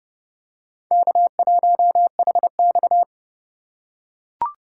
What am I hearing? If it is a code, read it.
Morse code, heard as a keyed tone, about 30 wpm, 700 Hz. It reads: K1HX
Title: USA Call Signs - 4 Characters - 3x Character Spacing - 30wpm